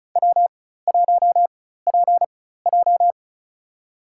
Morse code, heard as a keyed tone, about 36 wpm, spaced as for 16 wpm, 700 Hz. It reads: W1PJ
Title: USA Call Signs - 4 Characters - 4x Character Spacing - 35wpm